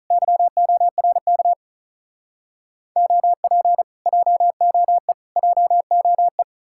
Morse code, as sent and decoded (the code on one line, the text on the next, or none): -.-- --- .-. -.- / --- .--. / .--- --- . / .--- --- .
YORK OP JOE JOE